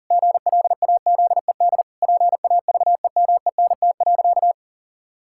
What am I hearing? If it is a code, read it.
Morse code, heard as a keyed tone, about 40 wpm, 700 Hz. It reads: GLAZED PAVEMENT.
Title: Pangram Challenge v2 - 40wpm